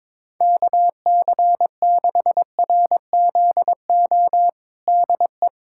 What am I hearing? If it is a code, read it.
Morse code, heard as a keyed tone, about 22 wpm, 700 Hz. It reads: KC6RZO DE
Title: USA Callsign DE Callsign - 6 Characters - 22wpm